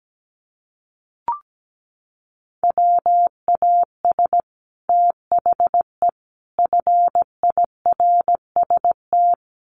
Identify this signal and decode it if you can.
Morse code, heard as a keyed tone, about 17 wpm, 700 Hz. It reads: WAS THE FIRST
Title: Sets of 3 Words 17wpm